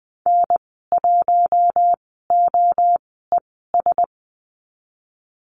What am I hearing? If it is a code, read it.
Morse code, heard as a keyed tone, about 20 wpm, spaced as for 15 wpm, 700 Hz. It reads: N1OES
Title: 048 Lesson 9g - Number 4 - Call Signs 20wpm